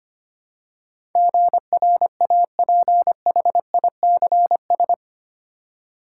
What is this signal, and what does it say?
Morse code, heard as a keyed tone, about 25 wpm, 690 Hz. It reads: GRAPHICS
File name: 8 Letter Words - First 3 Letters Given - 25wpm